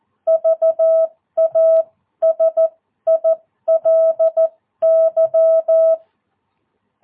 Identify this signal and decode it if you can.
Morse code, heard as a keyed tone, about 14 wpm, spaced as for 12 wpm, 660 Hz. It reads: VASILY
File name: jh8cbh / 2022cw